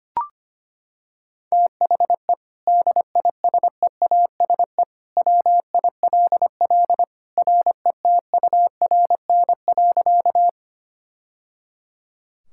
Morse code, heard as a keyed tone, about 25 wpm, 700 Hz. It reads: THE DISEASE WILL RETURN.
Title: Sentences from Top 1000 Words - Review Part 6 - 25wpm